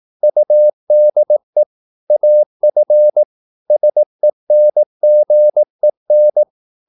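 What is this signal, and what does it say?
Morse code, heard as a keyed tone, about 18 wpm, 600 Hz. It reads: UDE AF SENGEN
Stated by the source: Slow (QRS) Morse practice at 18 wpm, CW med Gnister.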